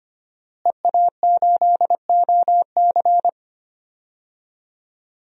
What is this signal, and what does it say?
Morse code, heard as a keyed tone, about 25 wpm, 700 Hz. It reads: EA8OC
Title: DXCC-100 Call Signs - 5 Characters - Repeated 3x - 25wpm